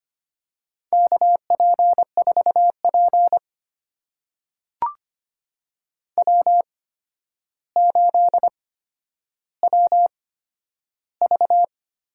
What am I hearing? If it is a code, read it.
Morse code, heard as a keyed tone, about 25 wpm, spaced as for 11 wpm, 700 Hz. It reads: KP4P W8WV
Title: USA Call Signs - 4 Characters - 8x Character Spacing - 25wpm